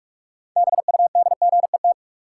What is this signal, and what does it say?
Morse code, heard as a keyed tone, about 46 wpm, 700 Hz. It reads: BUDGET
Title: Sentences from Top 1200 Words - New Word Drill Alt - 45wpm